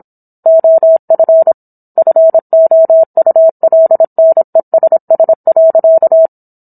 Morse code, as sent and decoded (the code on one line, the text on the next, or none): --- ..-. / ..-. --- ..- .-.. -. ... ... .-.-.-
OF FOULNESS.